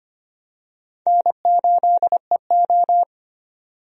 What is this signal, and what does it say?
Morse code, heard as a keyed tone, about 25 wpm, 700 Hz. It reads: N8EO
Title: USA Call Signs - 4 Characters - 25wpm